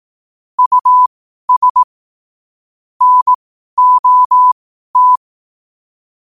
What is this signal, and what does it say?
Morse code, heard as a keyed tone, about 18 wpm, spaced as for 12 wpm, 1000 Hz. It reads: US NOT
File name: 12wpm